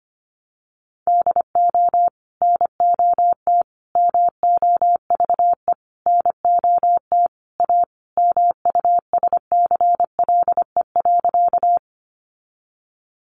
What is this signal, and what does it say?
Morse code, heard as a keyed tone, about 25 wpm, 700 Hz. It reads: DO NOT MOVE NOT A MUSCLE.
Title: Sentences from Top 1300 Words 25wpm